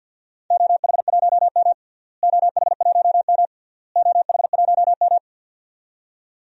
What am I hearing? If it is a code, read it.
Morse code, heard as a keyed tone, about 48 wpm, 700 Hz. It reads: OH1M OH1M OH1M
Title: DXCC-100 Call Signs - 4 Characters - Repeated 3x - 50wpm